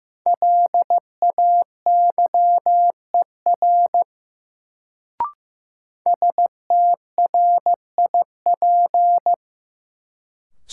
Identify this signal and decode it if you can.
Morse code, heard as a keyed tone, about 16 wpm, 700 Hz. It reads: LAYER STRIP
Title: Sentences from Top 1600 Words - New Word Drill Alt - 15wpm